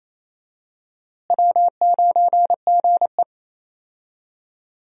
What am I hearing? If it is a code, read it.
Morse code, heard as a keyed tone, about 28 wpm, 700 Hz. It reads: W9GE
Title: USA Call Signs - 4 Characters - Repeated 2x - 28wpm